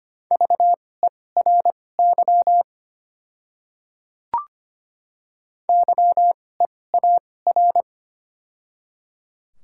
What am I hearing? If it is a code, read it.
Morse code, heard as a keyed tone, about 25 wpm, spaced as for 18 wpm, 690 Hz. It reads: VERY YEAR
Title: 2x Character Spacing - Top 300 Words - 25wpm